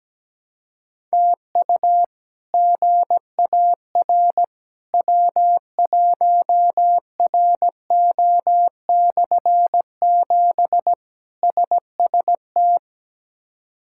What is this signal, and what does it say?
Morse code, heard as a keyed tone, about 17 wpm, 700 Hz. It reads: TU GAR W1RO/7 SST
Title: K1USN SST QSO Practice - 17wpm